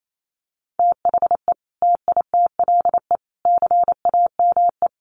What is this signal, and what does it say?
Morse code, heard as a keyed tone, about 28 wpm, 700 Hz. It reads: THE TITLE CAME
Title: Sentences from Top 1200 Words 28wpm